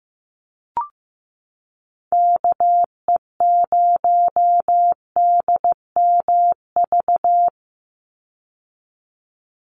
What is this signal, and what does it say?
Morse code, heard as a keyed tone, about 15 wpm, 700 Hz. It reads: KE0DMV
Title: USA Call Signs 15wpm